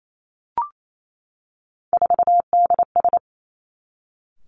Morse code, heard as a keyed tone, about 28 wpm, 700 Hz. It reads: 4DS
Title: Three Challenging Letter-Number Combinations 28wpm